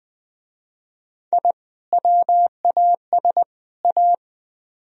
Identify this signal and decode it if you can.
Morse code, heard as a keyed tone, about 20 wpm, 700 Hz. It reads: I WAS A
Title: Speed-Racing - Sets of 3 Words 17wpm